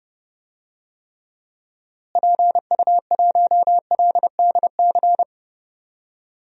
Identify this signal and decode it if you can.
Morse code, heard as a keyed tone, about 30 wpm, 700 Hz. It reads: PU1LDC
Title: DXCC-100 Call Signs - 6 Characters - Repeated 2x - 30wpm